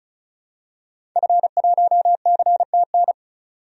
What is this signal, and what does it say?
Morse code, heard as a keyed tone, about 34 wpm, 700 Hz. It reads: F1CTN
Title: DXCC-100 Call Signs - 5 Characters - 35wpm